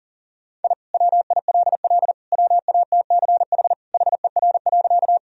Morse code, heard as a keyed tone, about 40 wpm, 700 Hz. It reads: I WILL WATCH HER.
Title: Sentences from Top 1000 Words - Review Part 1 - 40wpm